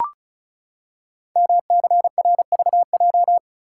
Morse code, heard as a keyed tone, about 36 wpm, 690 Hz. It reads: MCRVJ
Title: Code Groups - 5 Letters - 35wpm